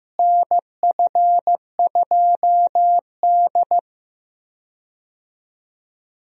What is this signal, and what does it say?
Morse code, heard as a keyed tone, about 15 wpm, 700 Hz. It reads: NF2D
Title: USA Call Signs - 4 Characters - Repeated 3x - 15wpm